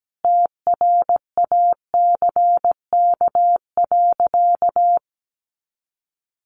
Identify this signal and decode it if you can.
Morse code, heard as a keyed tone, about 17 wpm, 700 Hz. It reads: TRACK.
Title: Sentences from Top 1100 Words 17wpm